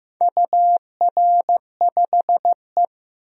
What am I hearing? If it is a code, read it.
Morse code, heard as a keyed tone, about 15 wpm, 700 Hz. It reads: UR5E
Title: DXCC-100 Call Signs - 4 Characters - 15wpm